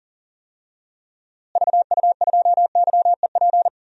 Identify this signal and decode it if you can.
Morse code, heard as a keyed tone, about 37 wpm, 700 Hz. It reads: VU2YEP
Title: DXCC-100 Call Signs - 6 Characters - Repeated 2x - 40wpm